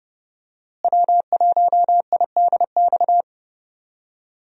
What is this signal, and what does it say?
Morse code, heard as a keyed tone, about 30 wpm, 700 Hz. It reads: W1IDX
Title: USA Call Signs - 5 Characters - Repeated 2x - 30wpm